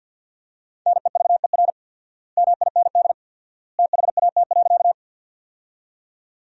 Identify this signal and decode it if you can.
Morse code, heard as a keyed tone, about 48 wpm, 700 Hz. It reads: NEVER MIND THAT.
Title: 4x Word-Spacing - Sentences from Top 500 Words - 50wpm